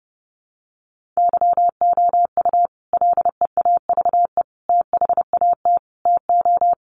Morse code, heard as a keyed tone, about 30 wpm, 700 Hz. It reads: YOU LEAVE THAT TO
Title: Sentences from Top 1400 Words 30wpm